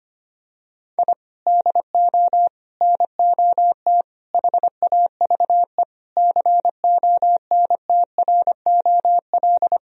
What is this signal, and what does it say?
Morse code, heard as a keyed tone, about 25 wpm, 700 Hz. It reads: I DO NOT HAVE CONTROL
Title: Sentences from Top 1000 Words - Review Part 3 - 25wpm